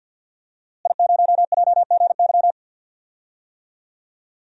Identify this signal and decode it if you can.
Morse code, heard as a keyed tone, about 51 wpm, 700 Hz. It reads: I0JGY
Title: DXCC-100 Call Signs - 5 Characters - Repeated 2x - 50wpm